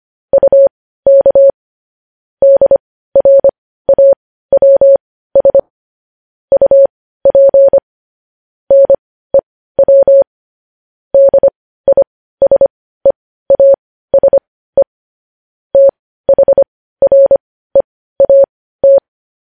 Morse code, heard as a keyed tone, about 25 wpm, spaced as for 15 wpm, 550 Hz. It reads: UK DRAWS UP NEW DISEASE THREAT